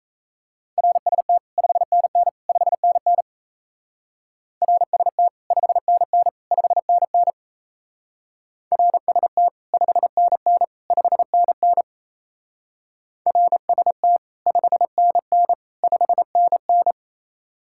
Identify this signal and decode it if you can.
Morse code, heard as a keyed tone, about 34 wpm, 700 Hz. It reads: RST 5NN 5NN RST 5NN 5NN RST 5NN 5NN RST 5NN 5NN
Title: Speed-Racing Common QSO Phrases - 28wpm